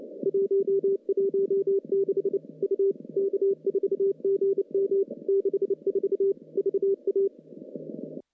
Morse code, heard as a keyed tone, about 28 wpm, 390 Hz. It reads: 116 U K4GM 64 VA
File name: CW-SS-k5tr